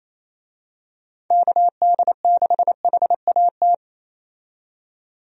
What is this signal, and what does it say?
Morse code, heard as a keyed tone, about 28 wpm, 700 Hz. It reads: KD6HAT
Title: USA Call Signs - 6 Characters - 28wpm